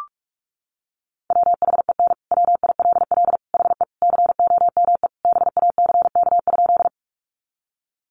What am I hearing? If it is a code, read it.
Morse code, heard as a keyed tone, about 44 wpm, 700 Hz. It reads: WHEN WILL HE COME BACK?
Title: Sentences from Top 200 Words 45wpm